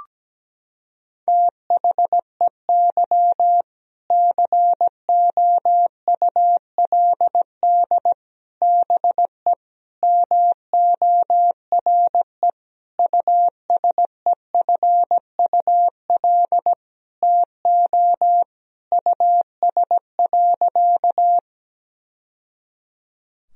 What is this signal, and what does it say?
Morse code, heard as a keyed tone, about 17 wpm, 700 Hz. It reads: THEY COULD BE MORE USEFUL TO US.
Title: Sentences from Top 1000 Words - Review Part 4 - 17wpm